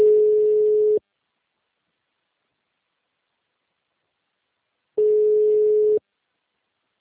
Ringback tone (Europe, 425 Hz).